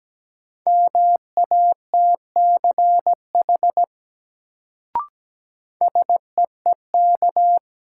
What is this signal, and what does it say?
Morse code, heard as a keyed tone, about 17 wpm, 700 Hz. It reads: MATCH SEEK
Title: Sentences from Top 1600 Words - New Word Drill - 17wpm